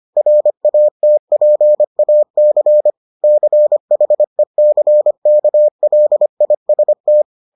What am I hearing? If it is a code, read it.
Morse code, heard as a keyed tone, about 25 wpm, 600 Hz. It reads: RATPAC CHECKLIST